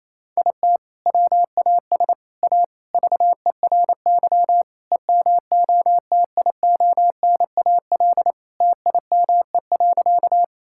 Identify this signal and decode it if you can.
Morse code, heard as a keyed tone, about 27 wpm, 700 Hz. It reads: IT WAS A VERY EMOTIONAL TIME.
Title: Sentences from Top 1000 Words - Review Part 5 - 28wpm